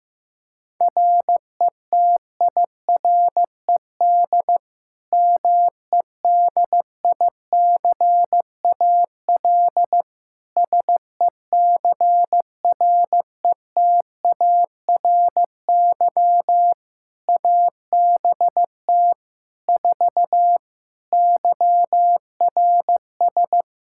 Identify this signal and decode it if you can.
Morse code, heard as a keyed tone, about 15 wpm, 700 Hz. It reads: RETIRED MEDICAL SECRETARY ABT 4 YRS
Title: Practice QSO - Exchange 3 - More Info - 15wpm